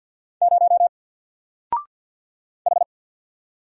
Morse code, heard as a keyed tone, about 50 wpm, 700 Hz. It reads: 0 H
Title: Single Character (Letter:Number:Punctuation:Prosign) 50wpm